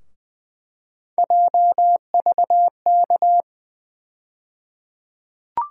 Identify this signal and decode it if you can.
Morse code, heard as a keyed tone, about 20 wpm, 700 Hz. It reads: JVK
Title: USA Call Sign Postfixes - 3 Letters 20wpm